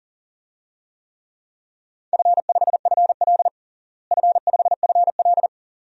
Morse code, heard as a keyed tone, about 40 wpm, 700 Hz. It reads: F5FL F5FL
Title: DXCC-100 Call Signs - 4 Characters - Repeated 2x - 40wpm